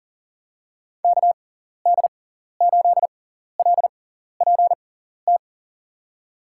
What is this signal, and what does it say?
Morse code, heard as a keyed tone, about 40 wpm, spaced as for 14 wpm, 700 Hz. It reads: KD8LPT
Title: USA Call Signs - 6 Characters - 6x Character Spacing - 40wpm